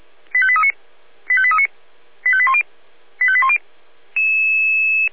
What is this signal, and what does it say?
Five-tone selective call (ZVEI), heard as quick runs of five tones, tones of about 70 ms. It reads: 85839 85839 85810 85810